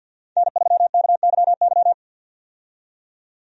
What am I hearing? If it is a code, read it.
Morse code, heard as a keyed tone, about 49 wpm, 700 Hz. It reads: N3KYY
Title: USA Call Signs - 5 Characters - 50wpm